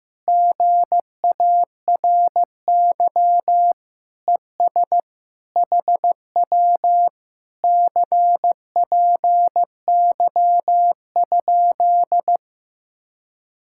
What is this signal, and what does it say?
Morse code, heard as a keyed tone, about 15 wpm, 700 Hz. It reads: GARY ES HW CPY?
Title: Speed-Racing Common QSO Phrases Encore - 15wpm